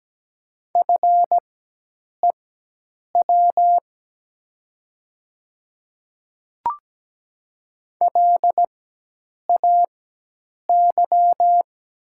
Morse code, heard as a keyed tone, about 17 wpm, spaced as for 8 wpm, 690 Hz. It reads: FEW LAY
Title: 4x Character Spacing - Top 500 Words - 17wpm